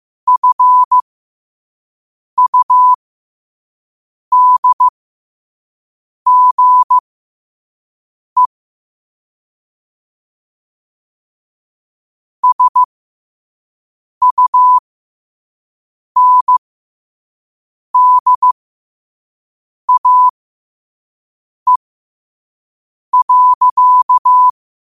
Morse code, heard as a keyed tone, about 15 wpm, spaced as for 5 wpm, 1000 Hz. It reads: FUDGE SUNDAE.
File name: qotdfiles / 5wpm